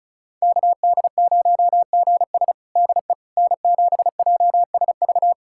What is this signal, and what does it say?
Morse code, heard as a keyed tone, about 36 wpm, 700 Hz. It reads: KD0GS DE N7JSV
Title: USA Callsign DE Callsign - 5 Characters - 35wpm